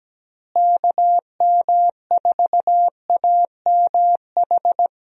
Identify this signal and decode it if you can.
Morse code, heard as a keyed tone, about 17 wpm, 700 Hz. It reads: KM4AMH
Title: USA Call Signs - 6 Characters - Repeated 3x - 17wpm